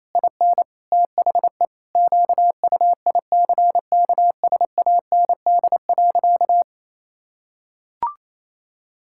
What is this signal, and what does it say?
Morse code, heard as a keyed tone, about 28 wpm, 700 Hz. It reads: IN THE QUICKSAND.
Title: Pangram Challenge v2 - 28wpm